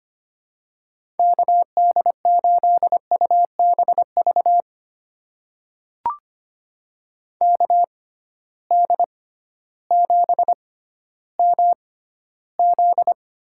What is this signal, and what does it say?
Morse code, heard as a keyed tone, about 25 wpm, spaced as for 14 wpm, 700 Hz. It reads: KD8UBV KD7MZ
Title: USA Call Signs - 6 Characters - 6x Character Spacing - 25wpm